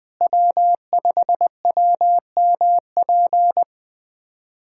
Morse code, heard as a keyed tone, about 20 wpm, 700 Hz. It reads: W5WMP
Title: USA Callsign DE Callsign - 5 Characters - 20wpm